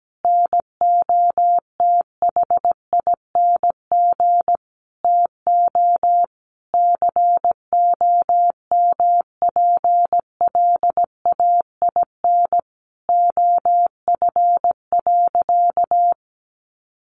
Morse code, heard as a keyed tone, about 17 wpm, 700 Hz. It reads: NOTHING TO COMPLAIN OF.
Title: Sentences from Top 1300 Words 17wpm